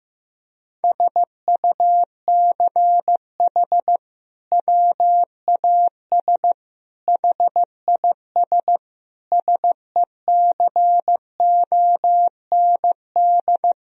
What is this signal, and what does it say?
Morse code, heard as a keyed tone, about 15 wpm, 700 Hz. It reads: SUCH WAS HIS SECOND
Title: Sentences from Top 1000 Words - Review Part 1 - 15wpm